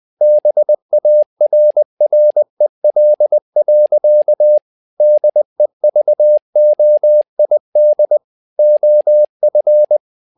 Morse code, heard as a keyed tone, about 20 wpm, 600 Hz. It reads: BARREL. DEVOID OF
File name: learning center / tower trouble / tower trouble 20wpm